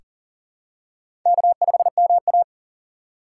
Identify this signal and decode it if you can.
Morse code, heard as a keyed tone, about 40 wpm, 700 Hz. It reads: K5MA
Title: K1USN SST QSO Practice - 40wpm